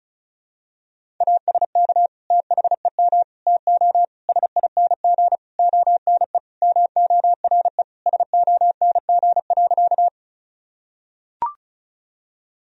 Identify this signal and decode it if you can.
Morse code, heard as a keyed tone, about 34 wpm, 700 Hz. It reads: ASK THEM TO SING ONE MORE SONG.
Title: Sentences from Top 1000 Words - Review Part 2 - 35wpm